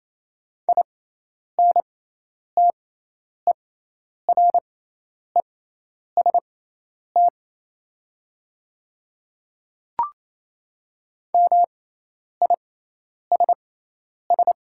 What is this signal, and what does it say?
Morse code, heard as a keyed tone, about 28 wpm, spaced as for 10 wpm, 700 Hz. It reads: INTEREST MISS
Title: 6x Character Spacing - Top 500 Words - 28wpm